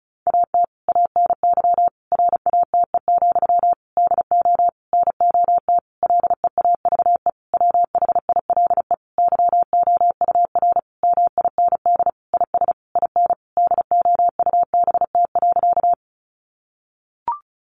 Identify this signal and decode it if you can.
Morse code, heard as a keyed tone, about 34 wpm, 700 Hz. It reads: AT ANY RATE, DO NOT LEAVE WHILE YOUR MIND IS IN DOUBT.
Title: Sentences from Top 1500 Words 35wpm